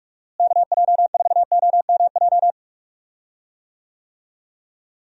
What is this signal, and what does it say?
Morse code, heard as a keyed tone, about 46 wpm, 700 Hz. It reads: KJ4OMJ
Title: USA Call Signs - 6 Characters - Repeated 2x - 45wpm